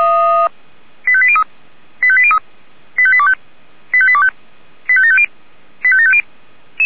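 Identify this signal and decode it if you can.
Five-tone selective call (ZVEI), heard as quick runs of five tones, tones of about 70 ms. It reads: 85803 85803 85825 85825 85750 85750
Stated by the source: ZVEI five-tone selective call received on an SDR.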